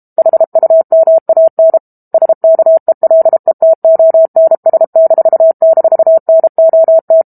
Morse code, heard as a keyed tone, about 32 wpm, 650 Hz. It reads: HUMAN SKELETONS--NOT
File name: Book Two / 06-Chapter 6 - 32 WPM